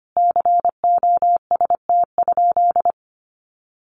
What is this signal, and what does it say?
Morse code, heard as a keyed tone, about 25 wpm, 700 Hz. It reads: COST?
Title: Sentences from Top 1400 Words 25wpm